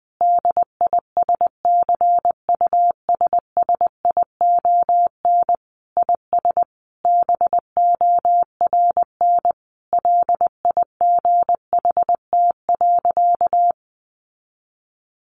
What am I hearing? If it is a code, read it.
Morse code, heard as a keyed tone, about 20 wpm, 700 Hz. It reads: DISCUSSION IS BORN LIGHT.
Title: Sentences from Top 1100 Words 20wpm